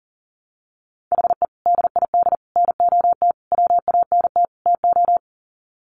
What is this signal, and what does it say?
Morse code, heard as a keyed tone, about 40 wpm, 700 Hz. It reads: HE DID NOT WANT TO
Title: Sets of 5 Words Encore 40wpm